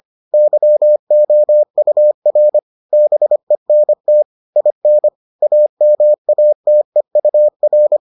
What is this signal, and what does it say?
Morse code, heard as a keyed tone, about 25 wpm, 600 Hz. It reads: YOUR BENT IN AMATEUR